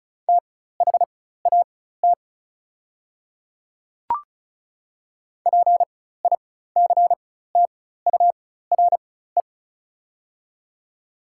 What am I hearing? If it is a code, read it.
Morse code, heard as a keyed tone, about 36 wpm, spaced as for 16 wpm, 700 Hz. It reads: THAT PICTURE
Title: 4x Character Spacing - Top 200 Words - 35wpm